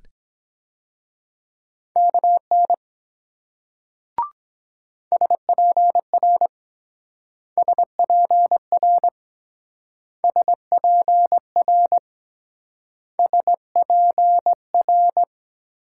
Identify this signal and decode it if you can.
Morse code, heard as a keyed tone, about 22 wpm, 700 Hz. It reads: KN SPR SPR SPR SPR